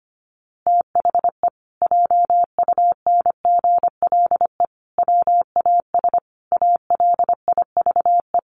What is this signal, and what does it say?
Morse code, heard as a keyed tone, about 25 wpm, 700 Hz. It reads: THE JUNGLE WAS ALIVE
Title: A Sound Of Thunder 25wpm